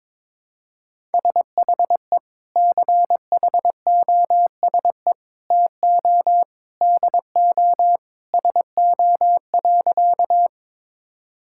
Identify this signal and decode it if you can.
Morse code, heard as a keyed tone, about 22 wpm, 700 Hz. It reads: SHE CHOSE TO DO SO.
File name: practice / 22wpm / Sentences with 3-Word Phrases - 22wpm